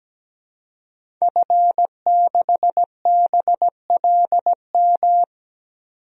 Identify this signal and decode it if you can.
Morse code, heard as a keyed tone, about 17 wpm, 700 Hz. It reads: F6BLM
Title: DXCC-100 Call Signs - 5 Characters - 17wpm